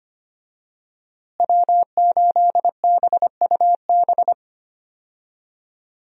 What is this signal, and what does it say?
Morse code, heard as a keyed tone, about 25 wpm, 700 Hz. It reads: W8BUB
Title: USA Call Signs - 5 Characters - Repeated 2x - 25wpm